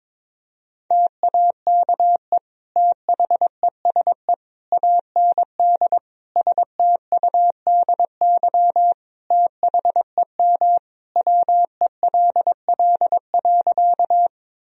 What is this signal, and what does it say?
Morse code, heard as a keyed tone, about 22 wpm, 700 Hz. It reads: TAKE THESE AND STUDY THEM WELL.
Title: Sentences from Top 1000 Words - Review Part 1 - 22wpm